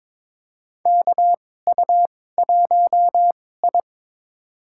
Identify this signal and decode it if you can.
Morse code, heard as a keyed tone, about 22 wpm, spaced as for 16 wpm, 700 Hz. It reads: KU1I
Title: USA Call Signs - 4 Characters - 2x Character Spacing - 22wpm